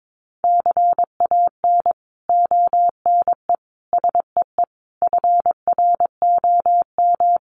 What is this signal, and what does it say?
Morse code, heard as a keyed tone, about 22 wpm, 700 Hz. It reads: CAN ONE SEE FROM
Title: Sentences from Top 100 Words 22wpm